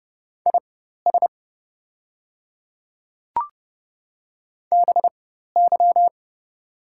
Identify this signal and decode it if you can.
Morse code, heard as a keyed tone, about 30 wpm, spaced as for 14 wpm, 690 Hz. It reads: IS BY
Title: 4x Character Spacing - Top 100 Words - 30wpm